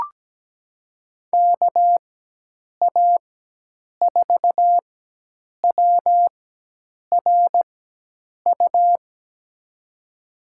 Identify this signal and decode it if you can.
Morse code, heard as a keyed tone, about 17 wpm, spaced as for 8 wpm, 700 Hz. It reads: KA4WRU
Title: USA Call Signs - 6 Characters - 4x Character Spacing - 17wpm